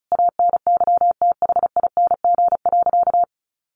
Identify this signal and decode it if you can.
Morse code, heard as a keyed tone, about 34 wpm, 700 Hz. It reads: ANYTHING.
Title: Sentences from Top 1500 Words 35wpm